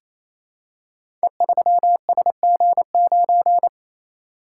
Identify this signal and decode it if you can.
Morse code, heard as a keyed tone, about 27 wpm, 700 Hz. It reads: E3SG9
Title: Code Groups - 5 Letter:Numbers - 28wpm